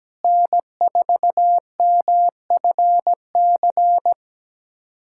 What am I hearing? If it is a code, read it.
Morse code, heard as a keyed tone, about 17 wpm, 700 Hz. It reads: N4MFC
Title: USA Callsign DE Callsign 17wpm